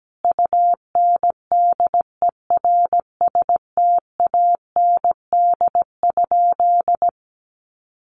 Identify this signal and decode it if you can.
Morse code, heard as a keyed tone, about 17 wpm, 700 Hz. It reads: UNDERSTAND?
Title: Sentences from Top 1100 Words 17wpm